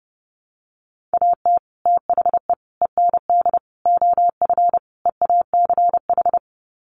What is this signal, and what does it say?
Morse code, heard as a keyed tone, about 30 wpm, 700 Hz. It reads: AT THE END OF EACH
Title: Sets of 5 Words Encore 30wpm